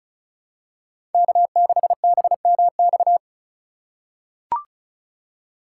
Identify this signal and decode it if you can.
Morse code, heard as a keyed tone, about 36 wpm, 700 Hz. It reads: K6BMX
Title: USA Call Signs - 5 Characters - 4x Character Spacing - 35wpm